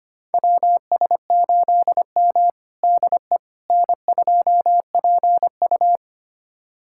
Morse code, heard as a keyed tone, about 25 wpm, 700 Hz. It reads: WS8M DE N2PU
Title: USA Callsign DE Callsign - 4 Characters - 25wpm